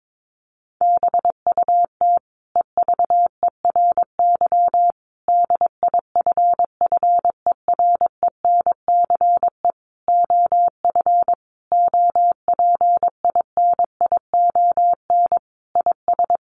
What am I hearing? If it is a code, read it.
Morse code, heard as a keyed tone, about 22 wpm, 700 Hz. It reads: BUT EVERY DIFFERENCE OF OPINION IS
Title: Sentences from Top 1400 Words 22wpm